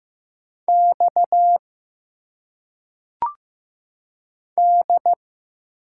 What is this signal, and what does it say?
Morse code, heard as a keyed tone, about 15 wpm, 700 Hz. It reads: X D
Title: Single Letters - Phonetic Pronunciation 15wpm